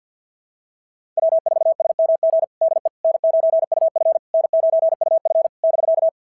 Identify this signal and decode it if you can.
Morse code, heard as a keyed tone, about 50 wpm, 660 Hz. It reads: W4SMG DE N9UF N9UF <BK>
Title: SOTA QSO Practice 50wpm